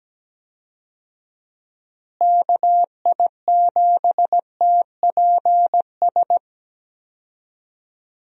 Morse code, heard as a keyed tone, about 17 wpm, 700 Hz. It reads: KI7TPS
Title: USA Call Signs - 6 Characters - Repeated 2x - 17wpm